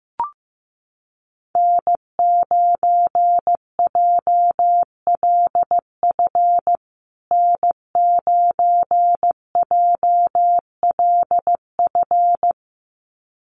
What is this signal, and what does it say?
Morse code, heard as a keyed tone, about 15 wpm, 700 Hz. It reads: N9JLF N9JLF
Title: USA Call Signs Repeated 2x 15wpm